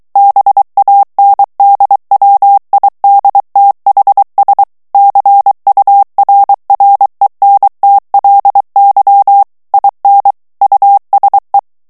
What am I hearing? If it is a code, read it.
Morse code, heard as a keyed tone, about 23 wpm, 800 Hz. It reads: BANDWIDTHS CURRENTLY IN USE
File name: ANGLAIS texte clair 3 vitesse 22 mots minute